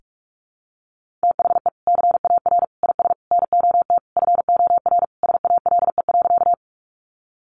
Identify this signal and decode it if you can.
Morse code, heard as a keyed tone, about 44 wpm, 700 Hz. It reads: THE CAR IS NOT FOR SALE.
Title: Sentences from Top 1400 Words 45wpm